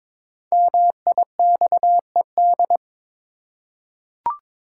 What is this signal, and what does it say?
Morse code, heard as a keyed tone, about 22 wpm, 700 Hz. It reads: MIXED
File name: 5 Letter Words - First 3 Letters Given - 22wpm